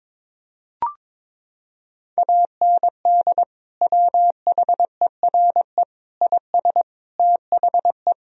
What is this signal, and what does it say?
Morse code, heard as a keyed tone, about 22 wpm, 700 Hz. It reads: AND WHERE IS THE
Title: Sentences from Top 1000 Words - Review Part 4 - 22wpm